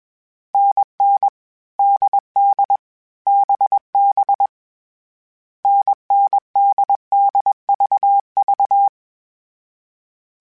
Morse code, heard as a keyed tone, about 21 wpm, 800 Hz. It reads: NN DD BB NNDDVV
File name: Short Runs / Short Runs-mp3s / NDB NDB6 @20